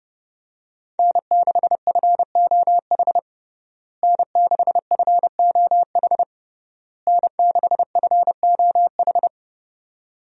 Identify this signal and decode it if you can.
Morse code, heard as a keyed tone, about 30 wpm, 700 Hz. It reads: N6FOH N6FOH N6FOH
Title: USA Call Signs - 5 Characters - Repeated 3x - 30wpm